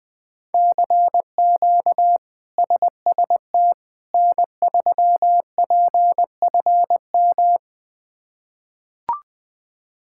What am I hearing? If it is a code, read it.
Morse code, heard as a keyed tone, about 20 wpm, 700 Hz. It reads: CQ SST N3PFM
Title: K1USN SST QSO Practice - 20wpm